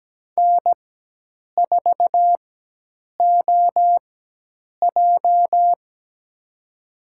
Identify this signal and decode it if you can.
Morse code, heard as a keyed tone, about 17 wpm, spaced as for 8 wpm, 700 Hz. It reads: N4OJ
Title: USA Call Signs - 4 Characters - 4x Character Spacing - 17wpm